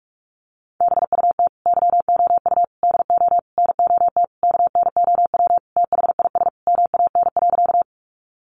Morse code, heard as a keyed tone, about 44 wpm, 700 Hz. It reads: BUT YOU DO NOT KNOW THIS MAN.
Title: Sentences from Top 600 Words 45wpm